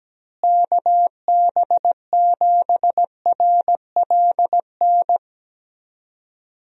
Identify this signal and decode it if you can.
Morse code, heard as a keyed tone, about 17 wpm, 700 Hz. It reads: KB7RLN
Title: USA Call Signs - 6 Characters - Repeated 2x - 17wpm